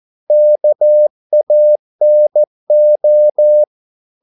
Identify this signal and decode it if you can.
Morse code, heard as a keyed tone, about 14 wpm, 600 Hz. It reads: KANO
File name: Korte ord DK 14wpm